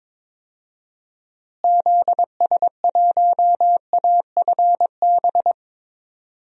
Morse code, heard as a keyed tone, about 21 wpm, 700 Hz. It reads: ZS1AFB
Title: DXCC-100 Call Signs - 6 Characters - Repeated 3x - 22wpm